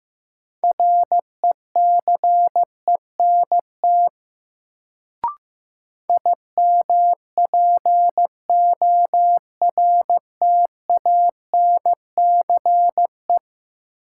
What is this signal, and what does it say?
Morse code, heard as a keyed tone, about 15 wpm, 700 Hz. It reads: RECENT IMPORTANCE
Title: Sets of 3 Words - New Word Drill - 15wpm